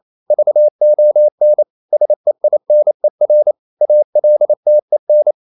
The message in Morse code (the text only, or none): VON SEINER ALTEN